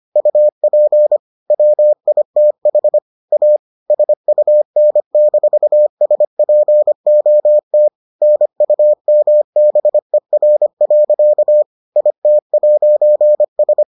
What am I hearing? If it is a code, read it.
Morse code, heard as a keyed tone, about 25 wpm, 600 Hz. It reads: UP WITH A SUN-SPOT NUMBER. IT'S